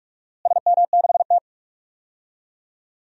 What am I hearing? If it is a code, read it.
Morse code, heard as a keyed tone, about 46 wpm, 700 Hz. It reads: SM6T
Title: DXCC-100 Call Signs - 4 Characters - Repeated 3x - 45wpm